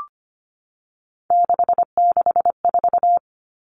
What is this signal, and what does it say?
Morse code, heard as a keyed tone, about 25 wpm, 700 Hz. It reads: 664
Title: Common Cut Numbers - 3-Digit 25wpm